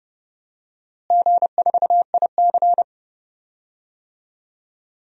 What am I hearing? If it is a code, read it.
Morse code, heard as a keyed tone, about 30 wpm, 700 Hz. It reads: G4IC
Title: DXCC-100 Call Signs - 4 Characters - Repeated 2x - 30wpm